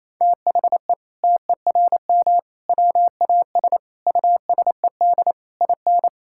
Morse code, heard as a keyed tone, about 28 wpm, 700 Hz. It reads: THE TERM WAS USED IN
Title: Sentences from Top 1000 Words - Review Part 6 - 28wpm